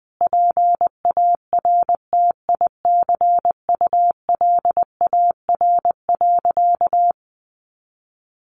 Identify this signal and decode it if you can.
Morse code, heard as a keyed tone, about 20 wpm, 700 Hz. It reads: PARTICULAR.
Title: Sentences from Top 900 Words 20wpm